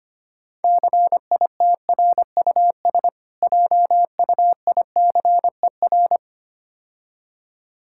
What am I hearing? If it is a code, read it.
Morse code, heard as a keyed tone, about 26 wpm, 700 Hz. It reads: CITRUS JUICER
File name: practice / 25wpm / Difficult Words - Speed-Racing with Hints - 25wpm